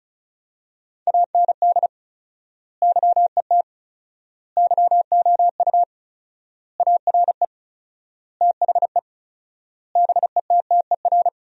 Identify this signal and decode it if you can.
Morse code, heard as a keyed tone, about 34 wpm, 700 Hz. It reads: AND YET YOU ARE THE BETTER
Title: 4x Word-Spacing - Sentences from Top 500 Words - 35wpm